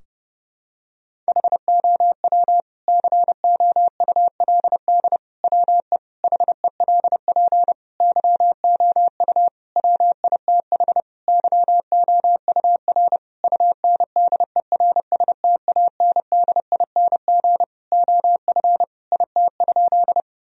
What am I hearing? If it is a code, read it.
Morse code, heard as a keyed tone, about 32 wpm, 700 Hz. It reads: HOW COULD WE HELP YOU WITH YOUR UNDERSTANDING OF IT?
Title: Sentences from Top 1000 Words - Review Part 3 - 30wpm